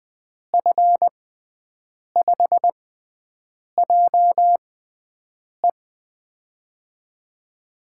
Morse code, heard as a keyed tone, about 20 wpm, spaced as for 7 wpm, 700 Hz. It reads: F5JE
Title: USA Call Signs - 5 Characters - 6x Character Spacing - 20wpm